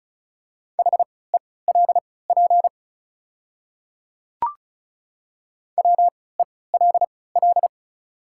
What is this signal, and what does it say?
Morse code, heard as a keyed tone, about 36 wpm, spaced as for 20 wpm, 700 Hz. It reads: HELP WELL
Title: 3x Character Spacing - Top 200 Words - 35wpm